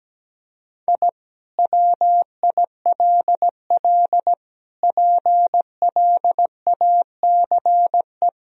Morse code, heard as a keyed tone, about 17 wpm, 700 Hz. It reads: I WILL PLACE
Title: Sentences from Top 1000 Words - Review Part 2 - 17wpm